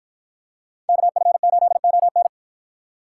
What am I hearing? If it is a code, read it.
Morse code, heard as a keyed tone, about 53 wpm, 700 Hz. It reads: KF8ON